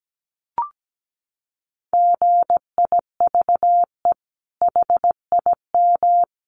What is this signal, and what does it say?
Morse code, heard as a keyed tone, about 17 wpm, 700 Hz. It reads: GIVE HIM
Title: Sentences from Top 600 Words 17wpm